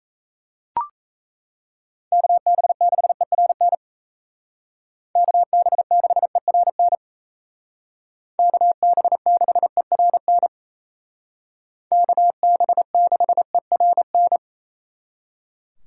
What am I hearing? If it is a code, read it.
Morse code, heard as a keyed tone, about 35 wpm, 700 Hz. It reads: KB6ERN KB6ERN KB6ERN KB6ERN